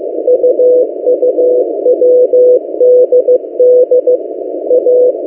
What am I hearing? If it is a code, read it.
Morse code, heard as a keyed tone, about 15 wpm, 500 Hz. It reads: UUWDD A
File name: M08a- -1355utc-8096khz